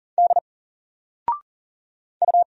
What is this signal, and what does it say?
Morse code, heard as a keyed tone, about 40 wpm, 700 Hz. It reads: D U
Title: Single Letter-Number Alt 40wpm